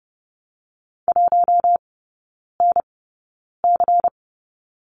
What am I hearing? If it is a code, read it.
Morse code, heard as a keyed tone, about 31 wpm, spaced as for 9 wpm, 700 Hz. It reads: 1NC
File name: Lesson 16e - Letter c - Three Characters 30wpm